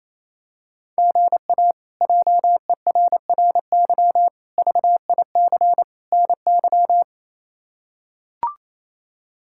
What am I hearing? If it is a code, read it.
Morse code, heard as a keyed tone, about 27 wpm, 700 Hz. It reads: GA JERRY VIC NY
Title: K1USN SST QSO Practice - 28wpm